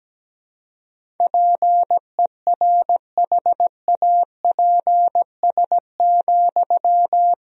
Morse code, heard as a keyed tone, about 17 wpm, 700 Hz. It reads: PERHAPS,
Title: Sentences from Top 1000 Words - Review Part 4 - 17wpm